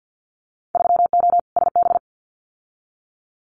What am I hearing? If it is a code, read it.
Morse code, heard as a keyed tone, about 50 wpm, 700 Hz. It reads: 3O SB
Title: Winter Field Day Exchanges Encore 50wpm